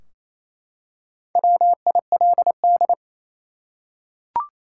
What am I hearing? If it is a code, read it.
Morse code, heard as a keyed tone, about 28 wpm, 700 Hz. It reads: WILD
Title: Four Letter Words - 28wpm